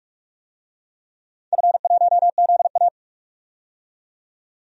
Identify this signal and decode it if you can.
Morse code, heard as a keyed tone, about 41 wpm, 700 Hz. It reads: F1ZA
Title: DXCC-100 Call Signs - 4 Characters - 45wpm